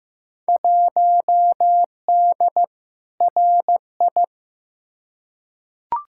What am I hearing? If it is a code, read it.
Morse code, heard as a keyed tone, about 15 wpm, 700 Hz. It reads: 1D RI
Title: Field Day Exchanges - 15wpm